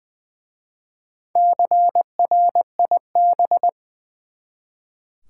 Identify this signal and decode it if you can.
Morse code, heard as a keyed tone, about 20 wpm, 700 Hz. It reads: CRIB
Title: Easy Words with Hints - 20wpm